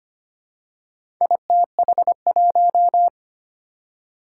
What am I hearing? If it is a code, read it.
Morse code, heard as a keyed tone, about 25 wpm, 700 Hz. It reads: ITH1
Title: Four Challenging Letter-Number Combinations 25wpm